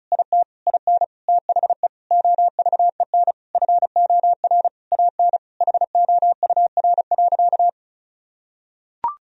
Morse code, heard as a keyed tone, about 36 wpm, 700 Hz. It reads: IT IN THE OVEN FOR AN HOUR.
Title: Sentences from Top 1000 Words - Review Part 4 - 35wpm